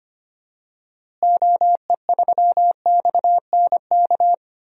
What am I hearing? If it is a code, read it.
Morse code, heard as a keyed tone, about 25 wpm, 700 Hz. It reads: OE3XNK
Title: DXCC-100 Call Signs - 6 Characters - Repeated 3x - 25wpm